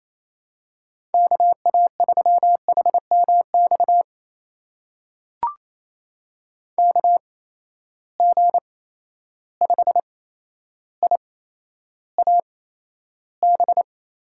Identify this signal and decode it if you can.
Morse code, heard as a keyed tone, about 28 wpm, spaced as for 12 wpm, 700 Hz. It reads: KA3HMX KG5IAB
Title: USA Call Signs - 6 Characters - 8x Character Spacing - 28wpm